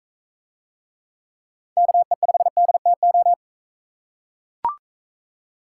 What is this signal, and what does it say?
Morse code, heard as a keyed tone, about 44 wpm, 700 Hz. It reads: KE5DTO